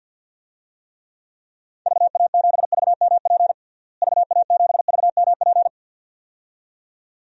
Morse code, heard as a keyed tone, about 52 wpm, 700 Hz. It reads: VA7VMP VA7VMP
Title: DXCC-100 Call Signs - 6 Characters - Repeated 2x - 50wpm